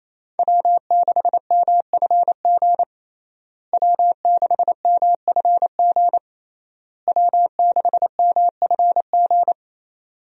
Morse code, heard as a keyed tone, about 28 wpm, 700 Hz. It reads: W6MFG W6MFG W6MFG
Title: USA Call Signs - 5 Characters - Repeated 3x - 28wpm